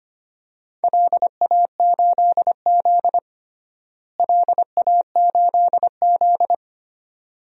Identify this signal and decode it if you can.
Morse code, heard as a keyed tone, about 25 wpm, 700 Hz. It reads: LA8Z LA8Z
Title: DXCC-100 Call Signs - 4 Characters - Repeated 2x - 25wpm